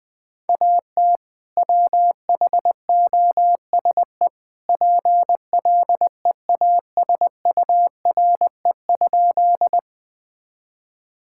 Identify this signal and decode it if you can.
Morse code, heard as a keyed tone, about 20 wpm, 700 Hz. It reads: AT WHOSE PLEASURE?
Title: Sentences from Top 1600 Words 20wpm